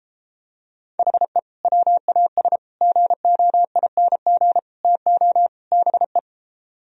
Morse code, heard as a keyed tone, about 33 wpm, 700 Hz. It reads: HE WAS GOING TO BE